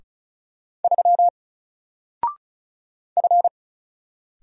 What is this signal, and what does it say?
Morse code, heard as a keyed tone, about 36 wpm, 700 Hz. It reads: 3 F
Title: Single Character (Letter:Number:Punctuation:Prosign) Alt 35wpm